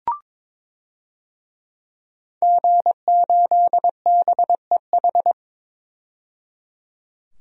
Morse code, heard as a keyed tone, about 22 wpm, 700 Hz. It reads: G8BEH
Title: DXCC-100 Call Signs - 5 Characters - 22wpm